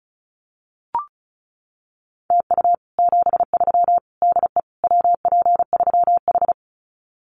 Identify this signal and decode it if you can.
Morse code, heard as a keyed tone, about 34 wpm, 700 Hz. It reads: TU 73 DE WP3H
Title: Common QSO Phrases Encore 35wpm